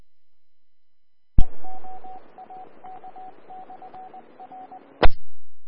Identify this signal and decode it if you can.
Morse code, heard as a keyed tone, about 20 wpm, 750 Hz. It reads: EOAK/R E
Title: Austin, TX Amateur Repeaters